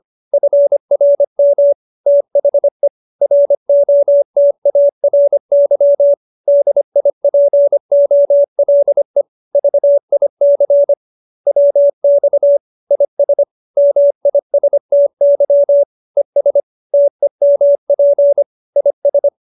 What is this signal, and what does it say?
Morse code, heard as a keyed tone, about 25 wpm, 570 Hz. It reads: FRM THE ROTARY DIPOLE VIC WX IS MISTY ES TEMP IS
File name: QSO315 25